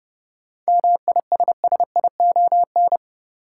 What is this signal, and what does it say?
Morse code, heard as a keyed tone, about 30 wpm, 700 Hz. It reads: MISSION
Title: Sentences from Top 1600 Words - New Word Drill Alt - 30wpm